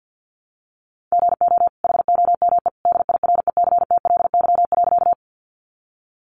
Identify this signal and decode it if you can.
Morse code, heard as a keyed tone, about 50 wpm, 700 Hz. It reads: GO HOME DIRECTLY.
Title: Sentences from Top 800 Words 50wpm